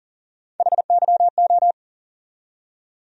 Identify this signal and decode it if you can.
Morse code, heard as a keyed tone, about 40 wpm, 700 Hz. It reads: HYO
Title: USA Call Sign Postfixes - 3 Letters 40wpm